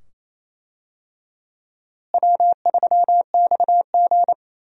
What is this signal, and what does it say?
Morse code, heard as a keyed tone, about 28 wpm, 700 Hz. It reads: W3XG
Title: USA Call Signs - 4 Characters - Repeated 2x - 28wpm